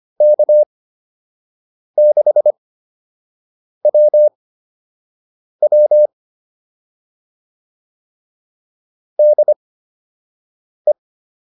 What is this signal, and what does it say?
Morse code, heard as a keyed tone, about 25 wpm, spaced as for 6 wpm, 600 Hz. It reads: K6WW DE
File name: QSO 204 06